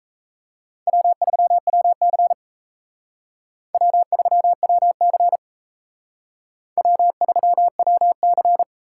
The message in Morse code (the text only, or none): W3WC W3WC W3WC